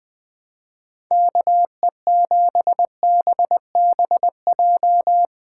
Morse code, heard as a keyed tone, about 20 wpm, 700 Hz. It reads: KE7BBJ
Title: USA Callsign DE Callsign 20wpm